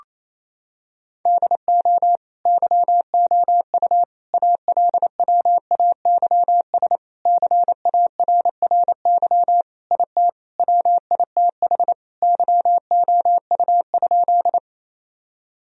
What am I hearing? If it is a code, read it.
Morse code, heard as a keyed tone, about 28 wpm, 700 Hz. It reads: DO YOU ALWAYS CARRY IT WITH YOU?
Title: Sentences from Top 1000 Words - Review Part 1 - 28wpm